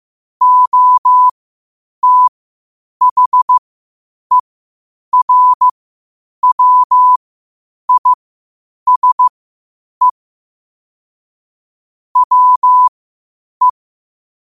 Morse code, heard as a keyed tone, about 15 wpm, spaced as for 8 wpm, 1000 Hz. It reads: OTHERWISE WE
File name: qotdfiles / 8wpm